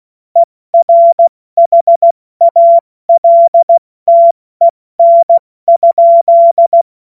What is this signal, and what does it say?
Morse code, heard as a keyed tone, about 16 wpm, spaced as for 14 wpm, 680 Hz. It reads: ERHALTEN?
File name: Morsekurs / Witze / Witz 07. Juni 2022 70 F80